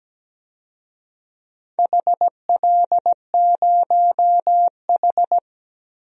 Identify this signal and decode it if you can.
Morse code, heard as a keyed tone, about 17 wpm, 700 Hz. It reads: HL0H
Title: DXCC-100 Call Signs - 4 Characters - 17wpm